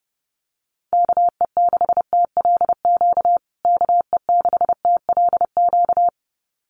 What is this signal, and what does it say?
Morse code, heard as a keyed tone, about 29 wpm, 700 Hz. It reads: KE6TLQ KE6TLQ
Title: USA Call Signs Repeated 2x 30wpm